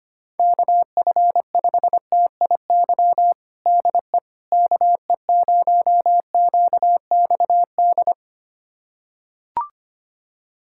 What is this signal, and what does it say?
Morse code, heard as a keyed tone, about 25 wpm, 700 Hz. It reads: KF5TIY DE KE0QXD
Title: USA Callsign DE Callsign - 6 Characters - 25wpm